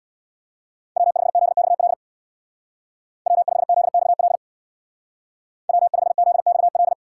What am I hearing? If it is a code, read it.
Morse code, heard as a keyed tone, about 65 wpm, 700 Hz. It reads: W5ZXL W5ZXL W5ZXL